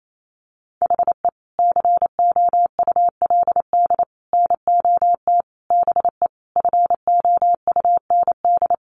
Morse code, heard as a keyed tone, about 28 wpm, 700 Hz. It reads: HE COULD NOT BE FOUND
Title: Sentences from Top 1400 Words 28wpm